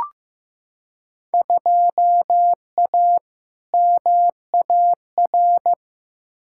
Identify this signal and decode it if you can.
Morse code, heard as a keyed tone, about 15 wpm, 700 Hz. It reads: 2A MAR
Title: Field Day Exchanges - 15wpm